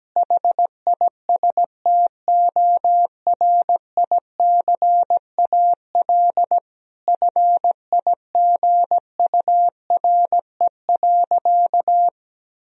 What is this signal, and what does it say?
Morse code, heard as a keyed tone, about 17 wpm, 700 Hz. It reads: HISTORICAL FIGURE.
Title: Sentences from Top 1000 Words - Review Part 4 - 17wpm